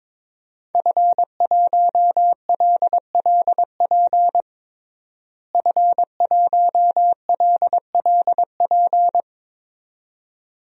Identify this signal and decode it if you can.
Morse code, heard as a keyed tone, about 22 wpm, 700 Hz. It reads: F1LLP F1LLP
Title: DXCC-100 Call Signs - 5 Characters - Repeated 2x - 22wpm